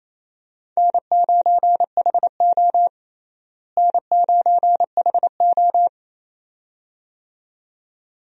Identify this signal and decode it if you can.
Morse code, heard as a keyed tone, about 28 wpm, 700 Hz. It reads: N9HO N9HO
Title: USA Call Signs - 4 Characters - Repeated 2x - 28wpm